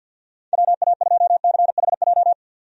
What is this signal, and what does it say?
Morse code, heard as a keyed tone, about 51 wpm, 700 Hz. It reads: WA2CHJ
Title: USA Call Signs - 6 Characters - 50wpm